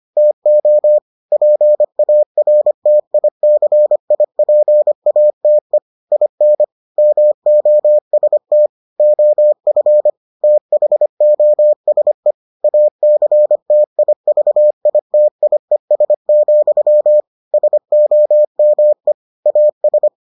Morse code, heard as a keyed tone, about 25 wpm, 600 Hz. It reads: TO PARTICIPATE IN MOST OF THOSE ACTIVITIES, SOME AS